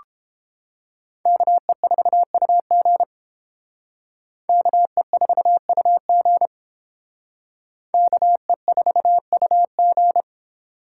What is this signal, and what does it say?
Morse code, heard as a keyed tone, about 30 wpm, 700 Hz. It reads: KE4UG KE4UG KE4UG